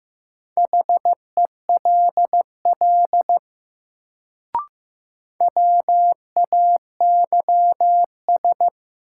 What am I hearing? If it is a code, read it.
Morse code, heard as a keyed tone, about 15 wpm, 700 Hz. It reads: HELL WAYS
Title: Sets of 5 Words Encore - New Word Drill - 15wpm